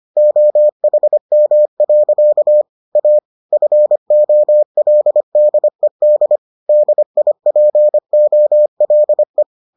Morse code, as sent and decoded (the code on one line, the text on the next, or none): --- .... -- .-.-.- / .- / ..-. --- .-.. -.. -.. / -.. .. .--. --- .-.. .
OHM. A FOLDED DIPOLE